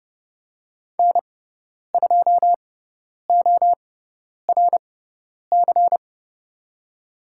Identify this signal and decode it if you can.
Morse code, heard as a keyed tone, about 30 wpm, spaced as for 10 wpm, 700 Hz. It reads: N2ORC
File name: Lesson 17g - Letter u - Call Signs 30wpm